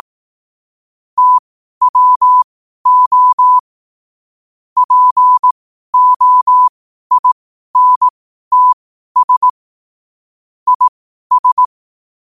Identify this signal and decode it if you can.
Morse code, heard as a keyed tone, about 18 wpm, spaced as for 12 wpm, 1000 Hz. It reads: TWO POINTS IS
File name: qotdfiles / 12wpm